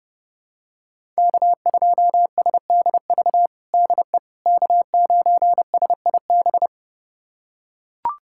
Morse code, heard as a keyed tone, about 30 wpm, 700 Hz. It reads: K2SDV DE K9SIB
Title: USA Callsign DE Callsign - 5 Characters - 30wpm